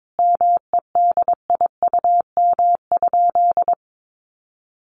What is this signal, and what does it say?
Morse code, heard as a keyed tone, about 22 wpm, 700 Hz. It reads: MEDIUM?
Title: Sentences from Top 900 Words 22wpm